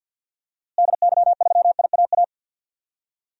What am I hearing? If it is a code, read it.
Morse code, heard as a keyed tone, about 52 wpm, 700 Hz. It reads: DY3IAA
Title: DXCC-100 Call Signs - 6 Characters - Repeated 3x - 50wpm